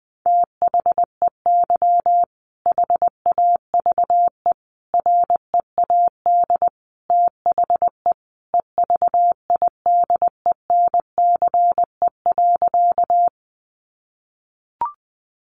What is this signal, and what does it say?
Morse code, heard as a keyed tone, about 20 wpm, 700 Hz. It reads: THEY HAVE READ THE EVIDENCE.
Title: Sentences from Top 1300 Words 20wpm